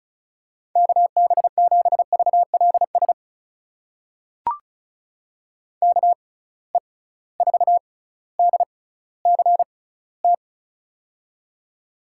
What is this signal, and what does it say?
Morse code, heard as a keyed tone, about 36 wpm, spaced as for 18 wpm, 690 Hz. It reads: KB7VLS KE4DCT
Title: USA Call Signs - 6 Characters - 6x Character Spacing - 35wpm